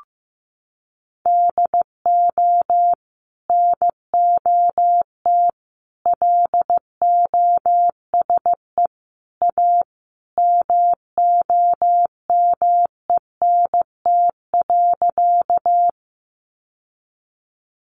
Morse code, heard as a keyed tone, about 15 wpm, 700 Hz. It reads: DO NOT LOSE A MOMENT.
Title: Sentences from Top 1100 Words 15wpm